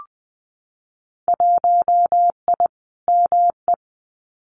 Morse code, heard as a keyed tone, about 20 wpm, 700 Hz. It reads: 1I ME
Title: Winter Field Day Exchanges 20wpm